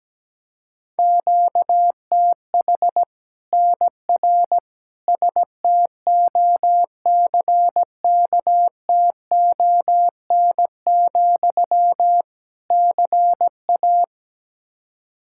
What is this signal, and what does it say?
Morse code, heard as a keyed tone, about 17 wpm, 700 Hz. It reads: QTH NR STOCKTON, CA
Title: Practice QSO - Exchange 1 - Beginning - 17wpm